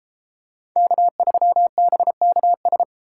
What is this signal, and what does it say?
Morse code, heard as a keyed tone, about 31 wpm, 700 Hz. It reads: K3BKS